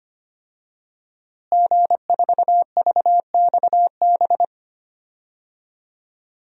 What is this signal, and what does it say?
Morse code, heard as a keyed tone, about 25 wpm, 700 Hz. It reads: G4VXB
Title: DXCC-100 Call Signs - 5 Characters - Repeated 2x - 25wpm